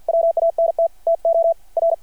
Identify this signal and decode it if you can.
Morse code, heard as a keyed tone, about 53 wpm, 670 Hz, with radio noise. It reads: WANT TO R
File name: archived / CRSnet